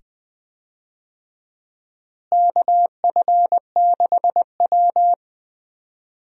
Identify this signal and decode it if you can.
Morse code, heard as a keyed tone, about 20 wpm, 700 Hz. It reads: KF6W
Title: USA Call Signs - 4 Characters - Repeated 2x - 20wpm